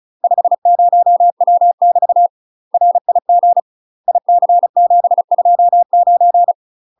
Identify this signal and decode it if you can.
Morse code, heard as a keyed tone, about 36 wpm, 700 Hz. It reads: 50W <BT> RIG IC729
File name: UK MASTER QSOs TRAINING / 35WPM / UK MASTER QSOs 0009